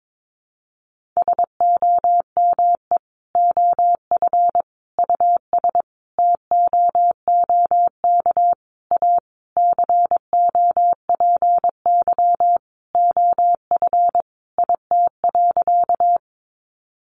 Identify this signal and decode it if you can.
Morse code, heard as a keyed tone, about 22 wpm, 700 Hz. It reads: SOME OF US TOOK A COPY OF IT.
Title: Sentences from Top 1400 Words 22wpm